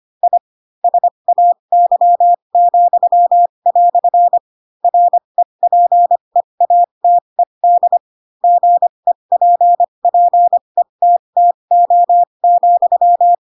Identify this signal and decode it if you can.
Morse code, heard as a keyed tone, about 25 wpm, spaced as for 22 wpm, 700 Hz. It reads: I SAY," REPEATED GEPPETTO,
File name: Pinocchio / Pinocchio ch7 18 25 ik1hge